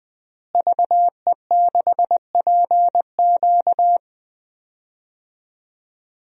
Morse code, heard as a keyed tone, about 20 wpm, 700 Hz. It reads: VE6PQ
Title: DXCC-100 Call Signs - 5 Characters - Repeated 3x - 20wpm